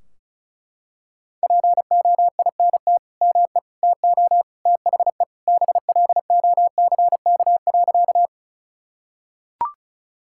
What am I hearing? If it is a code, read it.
Morse code, heard as a keyed tone, about 36 wpm, 700 Hz. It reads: POINT ME TO THE BLOCK.
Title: Sentences from Top 1600 Words 35wpm